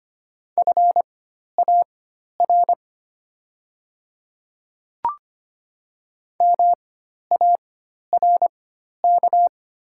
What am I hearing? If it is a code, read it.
Morse code, heard as a keyed tone, about 25 wpm, spaced as for 12 wpm, 700 Hz. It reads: FAR MARK
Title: 4x Character Spacing - Top 300 Words - 25wpm